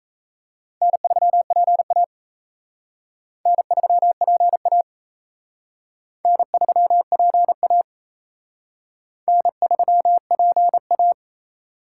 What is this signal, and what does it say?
Morse code, heard as a keyed tone, about 35 wpm, 700 Hz. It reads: N3PA N3PA N3PA N3PA
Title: Speed Racing - USA 1x2 Call Signs 28wpm